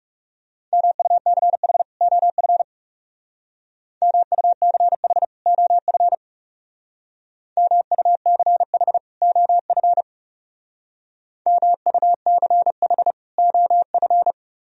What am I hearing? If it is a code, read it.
Morse code, heard as a keyed tone, about 35 wpm, 700 Hz. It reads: MUCH OF MUCH OF MUCH OF MUCH OF
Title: Speed-Racing - Sets of 2 Words 30wpm